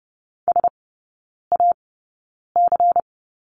Morse code, heard as a keyed tone, about 30 wpm, spaced as for 9 wpm, 700 Hz. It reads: SAC
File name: Lesson 16e - Letter c - Three Characters 30wpm